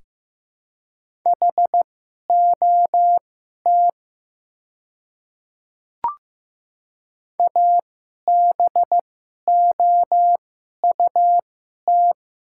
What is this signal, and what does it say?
Morse code, heard as a keyed tone, about 15 wpm, spaced as for 11 wpm, 700 Hz. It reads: HOT ABOUT
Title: 2x Character Spacing - Top 300 Words - 15wpm